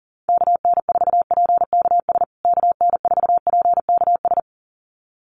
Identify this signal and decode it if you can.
Morse code, heard as a keyed tone, about 40 wpm, 700 Hz. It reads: KN4PKS KN4PKS
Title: USA Call Signs Repeated 2x 40wpm